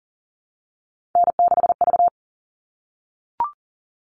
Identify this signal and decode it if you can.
Morse code, heard as a keyed tone, about 40 wpm, 700 Hz. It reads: N6V
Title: All Cut Numbers - 3-Digit 40wpm